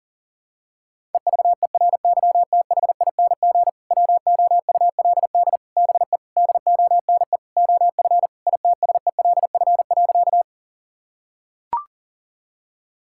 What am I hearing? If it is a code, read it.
Morse code, heard as a keyed tone, about 40 wpm, 700 Hz. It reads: EVERYTHING WOULD BE DONE OF ITSELF.
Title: Sentences from Top 1000 Words - Review Part 5 - 40wpm